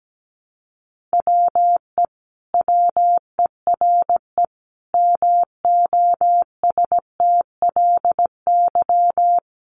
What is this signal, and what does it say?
Morse code, heard as a keyed tone, about 17 wpm, 700 Hz. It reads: WE WERE MOSTLY
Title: Sentences from Top 1300 Words 17wpm